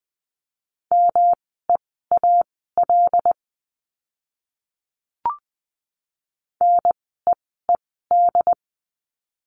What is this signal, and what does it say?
Morse code, heard as a keyed tone, about 20 wpm, spaced as for 14 wpm, 700 Hz. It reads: MEAL NEED
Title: 108 Lesson 18f - Letter m - Words 20wpm